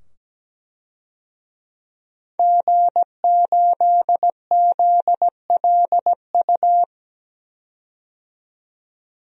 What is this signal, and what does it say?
Morse code, heard as a keyed tone, about 17 wpm, 700 Hz. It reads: G8ZLU
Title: DXCC-100 Call Signs - 5 Characters - 17wpm